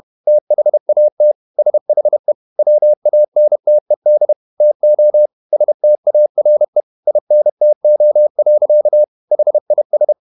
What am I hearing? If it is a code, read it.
Morse code, heard as a keyed tone, about 31 wpm, 600 Hz. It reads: THAT SHE WANTED TO STARE INTO. HIS
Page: Never 31wpm | CW med Gnister